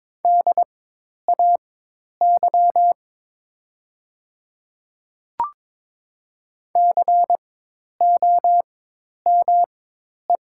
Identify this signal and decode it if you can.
Morse code, heard as a keyed tone, about 22 wpm, spaced as for 10 wpm, 700 Hz. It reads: DAY COME
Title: 4x Character Spacing - Top 100 Words - 22wpm